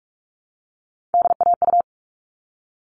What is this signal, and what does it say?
Morse code, heard as a keyed tone, about 44 wpm, 700 Hz. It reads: DAU
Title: All Cut Numbers - 3-Digit 45wpm